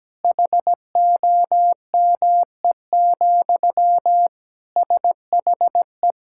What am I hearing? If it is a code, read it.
Morse code, heard as a keyed tone, about 17 wpm, 700 Hz. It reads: HOME, SHE
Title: Sentences from Top 1000 Words - Review Part 6 - 17wpm